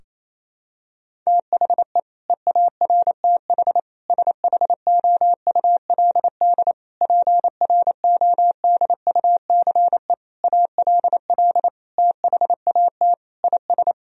Morse code, heard as a keyed tone, about 27 wpm, 700 Hz. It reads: THE EARTH SHOULD PRODUCE ALL THAT IS
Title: Sentences from Top 1000 Words - Review Part 3 - 28wpm